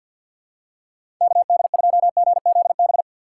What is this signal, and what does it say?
Morse code, heard as a keyed tone, about 52 wpm, 700 Hz. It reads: KD2GZB
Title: USA Call Signs - 6 Characters - 3x Character Spacing - 50wpm